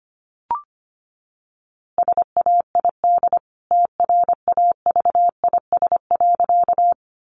Morse code, heard as a keyed tone, about 25 wpm, 700 Hz. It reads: SAID TRAVIS.
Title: A Sound Of Thunder 25wpm